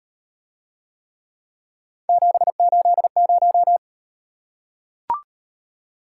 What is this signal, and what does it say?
Morse code, heard as a keyed tone, about 38 wpm, 700 Hz. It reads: 780